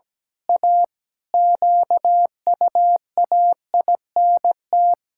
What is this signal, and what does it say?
Morse code, heard as a keyed tone, about 17 wpm, 700 Hz. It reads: A QUAINT
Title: Pangram Challenge v2 - 17wpm